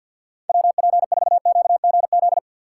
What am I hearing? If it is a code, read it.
Morse code, heard as a keyed tone, about 64 wpm, 700 Hz. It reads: WP4QGZ